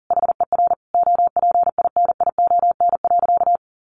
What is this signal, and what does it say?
Morse code, heard as a keyed tone, about 40 wpm, 700 Hz. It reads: HER OPINION.
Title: Sentences from Top 1400 Words 40wpm